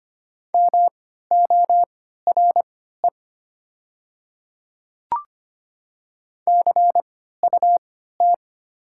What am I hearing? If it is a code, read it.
Morse code, heard as a keyed tone, about 25 wpm, spaced as for 14 wpm, 700 Hz. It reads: MORE CUT
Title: 3x Character Spacing - Top 300 Words - 25wpm